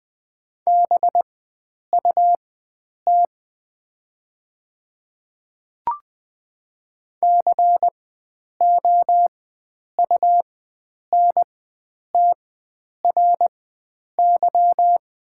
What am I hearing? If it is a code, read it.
Morse code, heard as a keyed tone, about 20 wpm, spaced as for 9 wpm, 700 Hz. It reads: BUT COUNTRY
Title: 4x Character Spacing - Top 300 Words - 20wpm